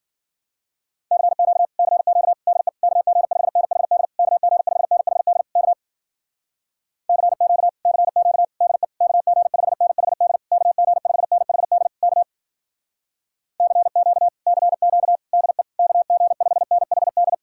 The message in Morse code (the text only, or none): CQ CQ DE KG5NHD KG5NHD K CQ CQ DE KG5NHD KG5NHD K CQ CQ DE KG5NHD